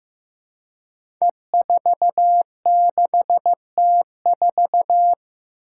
Morse code, heard as a keyed tone, about 15 wpm, 700 Hz. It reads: E46T4
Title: Five Challenging Letter-Number Combinations 15wpm